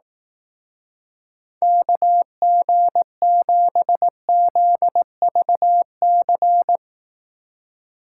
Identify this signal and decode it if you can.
Morse code, heard as a keyed tone, about 18 wpm, 700 Hz. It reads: KG7ZVC
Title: Speed Racing - USA 2x3 Call Signs 15wpm